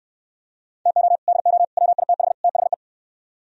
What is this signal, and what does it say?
Morse code, heard as a keyed tone, about 68 wpm, 700 Hz. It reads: TO DO WITH THE